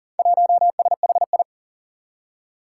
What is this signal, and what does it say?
Morse code, heard as a keyed tone, about 40 wpm, 700 Hz. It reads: 1SHI
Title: Four Challenging Letter-Number Combinations 40wpm